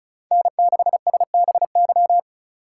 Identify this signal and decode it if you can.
Morse code, heard as a keyed tone, about 34 wpm, 700 Hz. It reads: N6SBY
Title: USA Call Signs - 2x Character Spacing - 35wpm